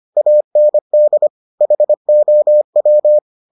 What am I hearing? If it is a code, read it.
Morse code, heard as a keyed tone, about 25 wpm, 600 Hz. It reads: AND HOW